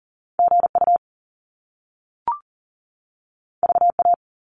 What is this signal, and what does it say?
Morse code, heard as a keyed tone, about 39 wpm, 700 Hz. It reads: GU VA
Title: Top Two Letter Combinations in Words 40wpm